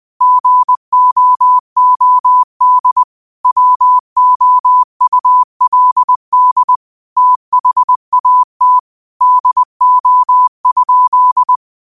Morse code, heard as a keyed tone, about 20 wpm, 1000 Hz. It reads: GOOD WOULD THAT DO?
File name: qotdfiles / 20wpm